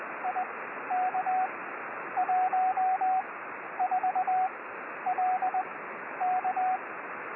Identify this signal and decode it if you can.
Morse code, heard as a keyed tone, about 20 wpm, spaced as for 11 wpm, 740 Hz, with radio noise. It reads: IK14LK
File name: 6V7SIX B